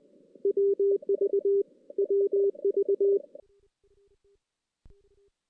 Morse code, heard as a keyed tone, about 20 wpm, 400 Hz. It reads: WV WV
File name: 2016 JAN CW NAQP / 2016 Jan naqp k5ot - 0-5